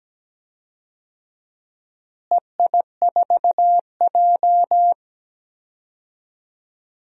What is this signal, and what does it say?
Morse code, heard as a keyed tone, about 17 wpm, 700 Hz. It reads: EI4J
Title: DXCC-100 Call Signs - 4 Characters - Repeated 3x - 17wpm